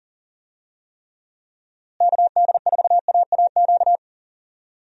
Morse code, heard as a keyed tone, about 40 wpm, 700 Hz. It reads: KD4AAQ
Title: USA Call Signs - 6 Characters - Repeated 3x - 40wpm